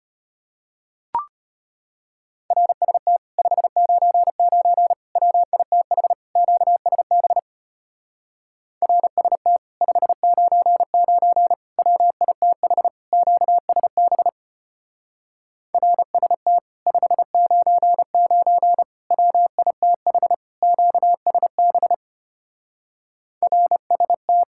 Morse code, heard as a keyed tone, about 31 wpm, 700 Hz. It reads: RST 599 WITH QSB RST 599 WITH QSB RST 599 WITH QSB RST
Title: Speed-Racing Common QSO Phrases - 25wpm